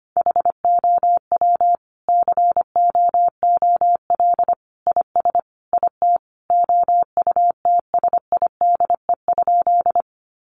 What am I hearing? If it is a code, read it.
Morse code, heard as a keyed tone, about 25 wpm, 700 Hz. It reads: HOW COOL IS IT OUTSIDE?
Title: Sentences from Top 1200 Words 25wpm